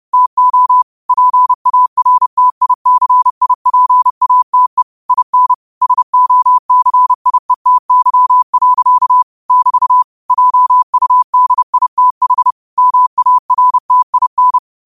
Morse code, heard as a keyed tone, about 30 wpm, 1000 Hz. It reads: TO PARTICIPATE IN SOCIETY. <BT> JUDITH MARTIN